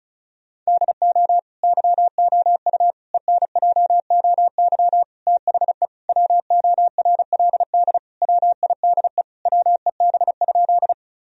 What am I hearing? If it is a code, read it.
Morse code, heard as a keyed tone, about 36 wpm, 700 Hz. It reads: DO YOU ENJOY THE WORLD WIDE WEB?
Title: Sentences from Top 1000 Words - Review Part 4 - 35wpm